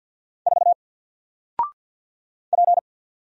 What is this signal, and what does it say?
Morse code, heard as a keyed tone, about 48 wpm, 700 Hz. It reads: 4 P
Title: Single Letter-Number Alt 50wpm